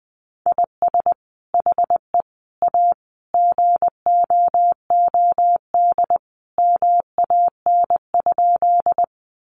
Morse code, heard as a keyed tone, about 20 wpm, 700 Hz. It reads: IS HE A GOOD MAN?
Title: Sentences from Top 200 Words 20wpm